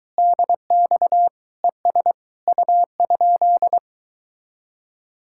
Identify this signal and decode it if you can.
Morse code, heard as a keyed tone, about 24 wpm, 700 Hz. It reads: DX ES U?